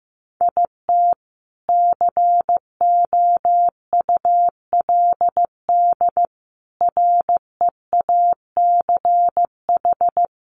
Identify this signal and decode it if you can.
Morse code, heard as a keyed tone, about 15 wpm, 700 Hz. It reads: IT COULD REACH
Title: A Sound Of Thunder 15wpm